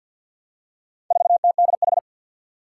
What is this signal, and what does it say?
Morse code, heard as a keyed tone, about 51 wpm, 700 Hz. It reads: 4TDH
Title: Four Challenging Letter-Number Combinations 50wpm